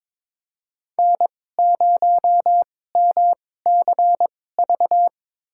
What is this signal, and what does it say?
Morse code, heard as a keyed tone, about 22 wpm, spaced as for 16 wpm, 700 Hz. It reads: N0MCV
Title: USA Call Signs - 2x Character Spacing - 22wpm